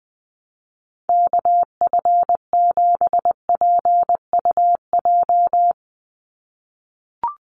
Morse code, heard as a keyed tone, about 20 wpm, 700 Hz. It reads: KF7PUJ
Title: USA Call Signs 20wpm